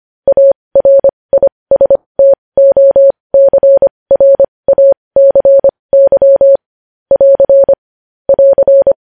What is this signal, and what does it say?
Morse code, heard as a keyed tone, about 25 wpm, spaced as for 20 wpm, 550 Hz. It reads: ARISTOCRACY <AR> <AR>
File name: podcast / News headlines-1x-25x20